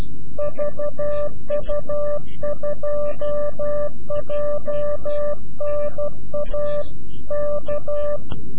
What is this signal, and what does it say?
Morse code, heard as a keyed tone, about 12 wpm, 590 Hz. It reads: VU2JNA K
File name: CW-QSO / CQ Call / CQ-QSO